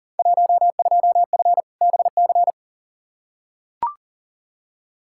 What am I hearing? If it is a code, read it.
Morse code, heard as a keyed tone, about 40 wpm, 700 Hz. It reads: 12F BC
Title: Field Day Exchanges Encore - 40wpm